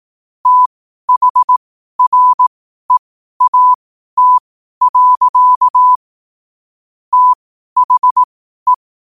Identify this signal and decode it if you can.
Morse code, heard as a keyed tone, about 18 wpm, spaced as for 12 wpm, 1000 Hz. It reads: THREAT. THE
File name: qotdfiles / 12wpm